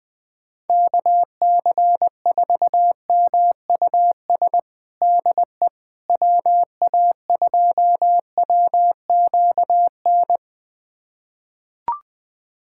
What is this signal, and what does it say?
Morse code, heard as a keyed tone, about 20 wpm, 700 Hz. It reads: KC4MUS DE WA2WQN
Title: USA Callsign DE Callsign 20wpm